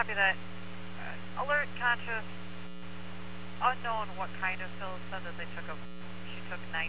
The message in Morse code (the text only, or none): none